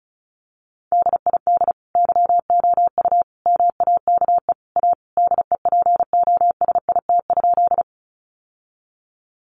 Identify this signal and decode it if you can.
Morse code, heard as a keyed tone, about 34 wpm, 700 Hz. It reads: DID YOU MAKE A DEPOSIT?
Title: Sentences from Top 1300 Words 35wpm